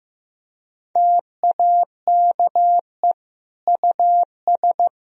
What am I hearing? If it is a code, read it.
Morse code, heard as a keyed tone, about 15 wpm, 700 Hz. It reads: TAKE US
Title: Sentences from Top 1000 Words - Review Part 1 - 15wpm